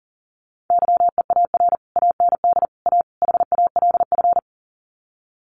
Morse code, heard as a keyed tone, about 40 wpm, 700 Hz. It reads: YEAR AND A HALF
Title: Sets of 4 Words 40wpm